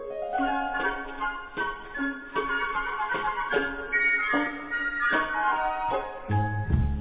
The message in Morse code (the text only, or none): none